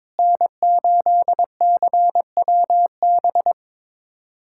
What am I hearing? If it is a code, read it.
Morse code, heard as a keyed tone, about 22 wpm, 700 Hz. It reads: N8CWB
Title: USA Callsign DE Callsign - 5 Characters - 22wpm